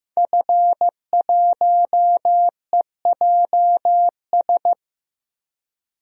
Morse code, heard as a keyed tone, about 15 wpm, 700 Hz. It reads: F1EJS
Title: DXCC-100 Call Signs - 5 Characters - Repeated 3x - 15wpm